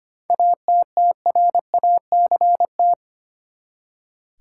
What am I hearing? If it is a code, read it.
Morse code, heard as a keyed tone, about 25 wpm, 700 Hz. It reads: ATTRACT
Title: Seven Letter Words - 25wpm